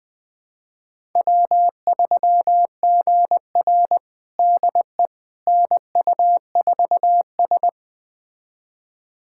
Morse code, heard as a keyed tone, about 20 wpm, 700 Hz. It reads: W3GR DE NU4S
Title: USA Callsign DE Callsign - 4 Characters - 20wpm